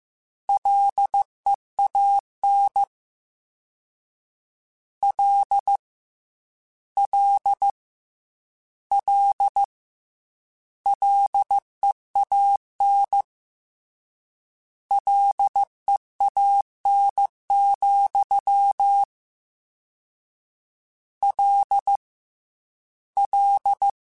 Morse code, heard as a keyed tone, about 15 wpm, 800 Hz. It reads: LEAN L L L LEAN LEAN, L L
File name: Combinations / Combinations-mp3s / Combinations@13